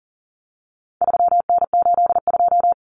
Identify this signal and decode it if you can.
Morse code, heard as a keyed tone, about 40 wpm, 700 Hz. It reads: 3N82
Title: Common Cut Numbers - 4-Digit 40wpm